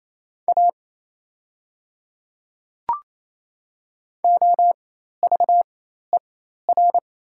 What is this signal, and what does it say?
Morse code, heard as a keyed tone, about 28 wpm, spaced as for 13 wpm, 700 Hz. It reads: A OVER
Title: 4x Character Spacing - Top 100 Words - 28wpm